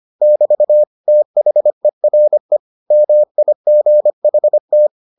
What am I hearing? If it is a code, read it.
Morse code, heard as a keyed tone, about 25 wpm, 600 Hz. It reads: <BT> THERE MIGHT